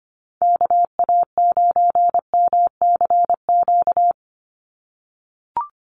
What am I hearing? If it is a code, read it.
Morse code, heard as a keyed tone, about 25 wpm, 700 Hz. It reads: KA9MCQ
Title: USA Call Signs 25wpm